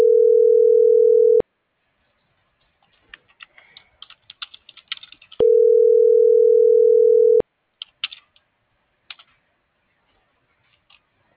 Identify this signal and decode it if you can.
Ringback tone (North America), 440 + 480 Hz.